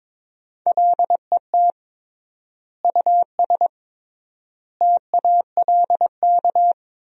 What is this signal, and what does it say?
Morse code, heard as a keyed tone, about 22 wpm, 700 Hz. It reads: LET US TALK
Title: 3x Word-Spacing - Sentences from Top 400 Words - 22wpm